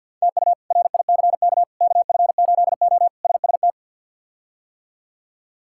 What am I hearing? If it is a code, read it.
Morse code, heard as a keyed tone, about 50 wpm, 700 Hz. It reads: TU RICK KF8O SST
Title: K1USN SST QSO Practice - 50wpm